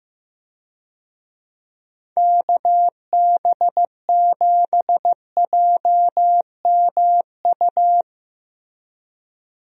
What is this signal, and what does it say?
Morse code, heard as a keyed tone, about 16 wpm, 700 Hz. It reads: KB7JMU
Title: USA Call Signs - 6 Characters - Repeated 3x - 15wpm